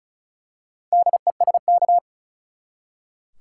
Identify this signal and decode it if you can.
Morse code, heard as a keyed tone, about 36 wpm, 700 Hz. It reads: DESK
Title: Four Letter Words - 35wpm